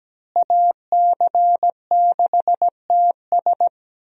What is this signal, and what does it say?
Morse code, heard as a keyed tone, about 17 wpm, 700 Hz. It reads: AC6TS
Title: USA Call Signs - 5 Characters - Repeated 2x - 17wpm